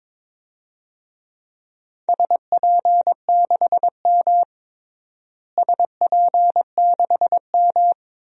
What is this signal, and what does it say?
Morse code, heard as a keyed tone, about 22 wpm, 700 Hz. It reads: SP6M SP6M
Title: DXCC-100 Call Signs - 4 Characters - Repeated 2x - 22wpm